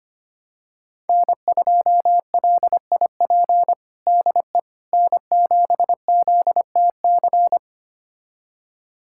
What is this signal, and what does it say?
Morse code, heard as a keyed tone, about 25 wpm, 700 Hz. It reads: N2LIP DE N7ZTC
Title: USA Callsign DE Callsign - 5 Characters - 25wpm